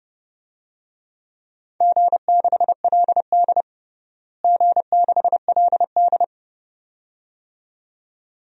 Morse code, heard as a keyed tone, about 30 wpm, 700 Hz. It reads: G6LD G6LD
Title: DXCC-100 Call Signs - 4 Characters - Repeated 2x - 30wpm